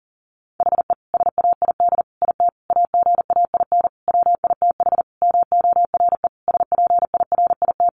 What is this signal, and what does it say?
Morse code, heard as a keyed tone, about 40 wpm, 700 Hz. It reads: HE SAID IT AGAIN WITH MORE SPIRIT
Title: Sentences from Top 1400 Words 40wpm